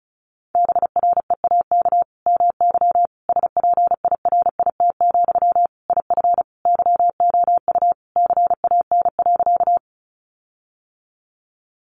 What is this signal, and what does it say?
Morse code, heard as a keyed tone, about 34 wpm, 700 Hz. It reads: BREAK MY SPIRIT, IF YOU CAN.
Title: Sentences from Top 1400 Words 35wpm